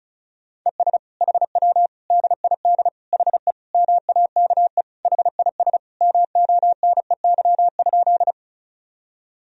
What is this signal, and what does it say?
Morse code, heard as a keyed tone, about 36 wpm, 700 Hz. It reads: ES HW DID HE MAKE HIS MONEY?
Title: Sentences from Top 400 Words and Abbreviations 35wpm